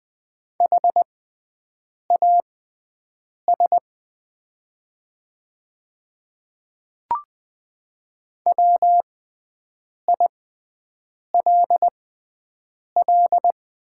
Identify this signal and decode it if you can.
Morse code, heard as a keyed tone, about 20 wpm, spaced as for 7 wpm, 700 Hz. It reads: HAS WILL
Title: 6x Character Spacing - Top 100 Words - 20wpm